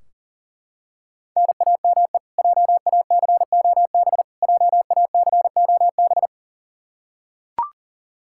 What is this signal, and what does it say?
Morse code, heard as a keyed tone, about 40 wpm, 700 Hz. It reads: NAME JACOB JACOB
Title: K1USN SST QSO Practice 40wpm